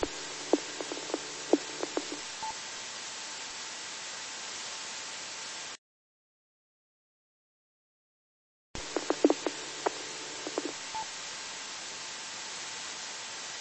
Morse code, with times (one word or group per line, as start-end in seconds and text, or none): none